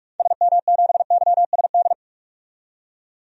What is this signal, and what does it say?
Morse code, heard as a keyed tone, about 41 wpm, 700 Hz. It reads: SM7YSD
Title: DXCC-100 Call Signs - 6 Characters - 45wpm